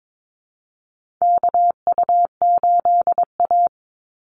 Morse code, heard as a keyed tone, about 22 wpm, 700 Hz. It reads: KU8A
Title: USA Call Signs 22wpm